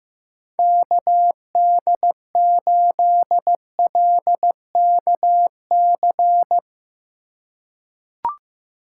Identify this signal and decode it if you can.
Morse code, heard as a keyed tone, about 16 wpm, 700 Hz. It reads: KD8LKC
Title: USA Call Signs - 6 Characters - 2x Character Spacing - 15wpm